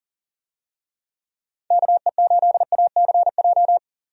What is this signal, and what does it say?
Morse code, heard as a keyed tone, about 40 wpm, 700 Hz. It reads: KE8ACJ
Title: USA Call Signs - 6 Characters - 40wpm